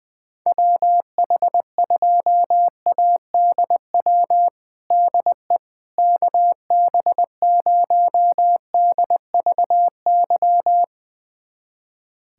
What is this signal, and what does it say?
Morse code, heard as a keyed tone, about 21 wpm, 700 Hz. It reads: WH2ADW DE KB0DVY
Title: USA Callsign DE Callsign - 6 Characters - 20wpm